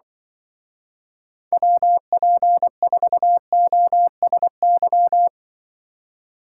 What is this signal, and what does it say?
Morse code, heard as a keyed tone, about 25 wpm, 700 Hz. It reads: WP4OSY